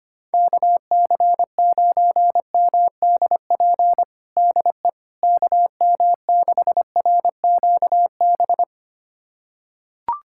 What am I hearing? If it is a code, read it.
Morse code, heard as a keyed tone, about 25 wpm, 700 Hz. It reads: KC9MDP DE KM6RQB
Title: USA Callsign DE Callsign - 6 Characters - 25wpm